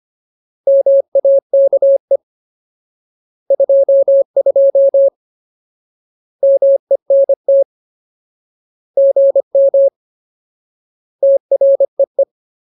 Morse code, heard as a keyed tone, about 25 wpm, 550 Hz. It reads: MAKE 22 MENT GM TREE
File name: Ultimate Word List / 25WPM Ultimate